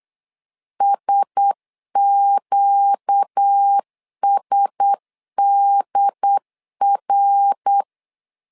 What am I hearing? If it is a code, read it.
Morse code, heard as a keyed tone, about 8 wpm, 800 Hz. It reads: SQSDR
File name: LEZIONI / 4 Q1S 8wpm 2710